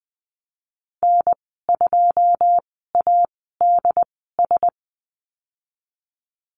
Morse code, heard as a keyed tone, about 20 wpm, spaced as for 15 wpm, 700 Hz. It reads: N2ADS
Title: 147 Lesson 24g - Letter y - Call Signs 20wpm